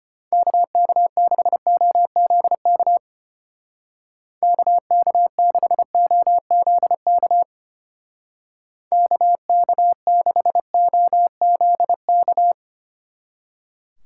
Morse code, heard as a keyed tone, about 31 wpm, 700 Hz. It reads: KK6OZK KK6OZK KK6OZK